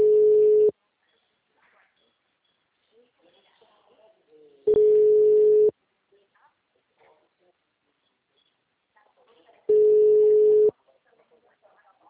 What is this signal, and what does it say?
Ringback tone (Europe, 425 Hz), with speech.